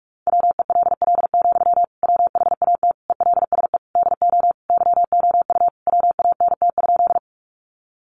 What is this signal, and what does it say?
Morse code, heard as a keyed tone, about 44 wpm, 700 Hz. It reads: WELL, WHAT ELSE DO YOU WANT?
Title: Sentences from Top 800 Words 45wpm